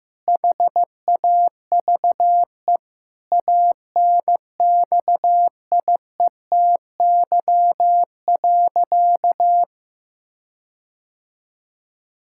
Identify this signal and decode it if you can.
Morse code, heard as a keyed tone, about 16 wpm, 700 Hz. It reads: HAVE ANXIETY.
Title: Sentences from Top 1600 Words 15wpm